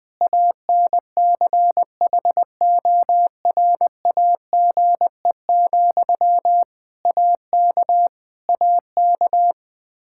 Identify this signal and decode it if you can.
Morse code, heard as a keyed tone, about 20 wpm, 700 Hz. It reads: ANCHORAGE, AK AK
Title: Practice QSO - Exchange 1 - Beginning - 20wpm